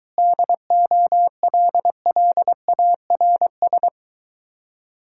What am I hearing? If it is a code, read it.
Morse code, heard as a keyed tone, about 23 wpm, 700 Hz. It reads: DOLLARS